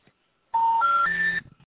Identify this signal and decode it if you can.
Special information tone.